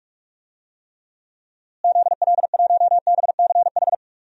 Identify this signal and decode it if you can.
Morse code, heard as a keyed tone, about 44 wpm, 700 Hz. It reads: ZL1BCH
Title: DXCC-100 Call Signs - 6 Characters - Repeated 2x - 45wpm